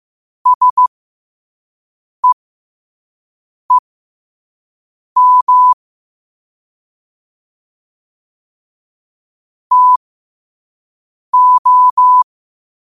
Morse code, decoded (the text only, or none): SEEM TO